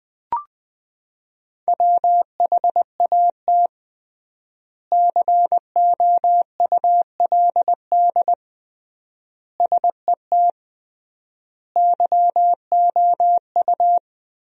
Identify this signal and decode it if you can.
Morse code, heard as a keyed tone, about 20 wpm, 700 Hz. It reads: WHAT COULD SET YOU
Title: 3x Word-Spacing - Sentences from Top 200 Words - 20wpm